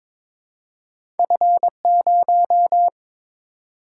Morse code, heard as a keyed tone, about 22 wpm, 700 Hz. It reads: F0
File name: practice / 22wpm / Two Letter-Number Combinations - 22wpm